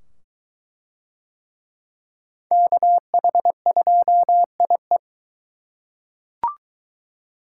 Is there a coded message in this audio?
Yes: Morse code from 2.5 to 5.0 s, heard as a keyed tone, about 23 wpm, 700 Hz. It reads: KH2IE